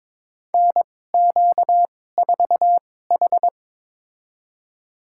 Morse code, heard as a keyed tone, about 22 wpm, spaced as for 16 wpm, 700 Hz. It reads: NQ4H
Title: USA Call Signs - 4 Characters - 2x Character Spacing - 22wpm